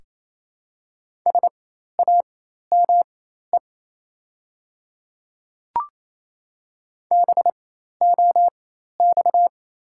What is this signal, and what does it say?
Morse code, heard as a keyed tone, about 28 wpm, spaced as for 13 wpm, 700 Hz. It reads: SAME BOX
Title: 4x Character Spacing - Top 500 Words - 28wpm